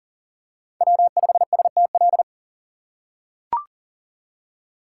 Morse code, heard as a keyed tone, about 40 wpm, 700 Hz. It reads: W5STL
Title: USA Call Signs - 5 Characters - 4x Character Spacing - 40wpm